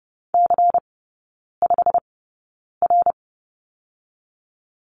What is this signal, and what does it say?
Morse code, heard as a keyed tone, about 30 wpm, spaced as for 9 wpm, 700 Hz. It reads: C5R
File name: Lesson 18e - Letter m - Three Characters 30wpm